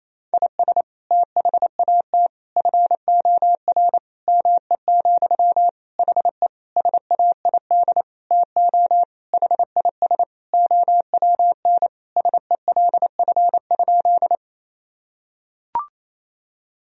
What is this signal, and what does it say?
Morse code, heard as a keyed tone, about 27 wpm, 700 Hz. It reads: IS THAT FOR ME, HE SAID TO HIS OWN SELF?
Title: Sentences from Top 1000 Words - Review Part 1 - 28wpm